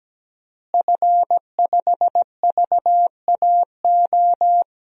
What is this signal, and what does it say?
Morse code, heard as a keyed tone, about 17 wpm, 700 Hz. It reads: F5VAO
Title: DXCC-100 Call Signs - 5 Characters - Repeated 3x - 17wpm